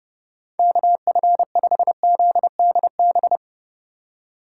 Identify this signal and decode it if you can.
Morse code, heard as a keyed tone, about 30 wpm, 700 Hz. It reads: KF5ZDB